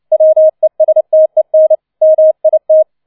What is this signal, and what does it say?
Morse code, heard as a keyed tone, about 29 wpm, 630 Hz. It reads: WESTEN MIT